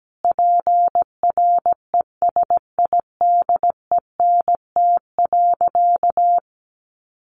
Morse code, heard as a keyed tone, about 17 wpm, 700 Hz. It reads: PRESIDENT.
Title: Sentences from Top 1000 Words 17wpm